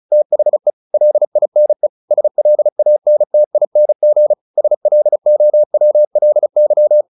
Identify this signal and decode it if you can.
Morse code, heard as a keyed tone, about 34 wpm, 600 Hz. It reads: THE LINE SLANTING SLOWLY
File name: the old man and the sea / the old man and the sea 1 35wpm